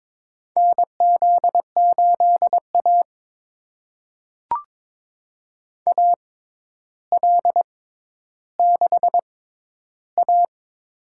Morse code, heard as a keyed tone, about 21 wpm, spaced as for 11 wpm, 700 Hz. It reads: NZ8A AL6A
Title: USA Call Signs - 4 Characters - 6x Character Spacing - 22wpm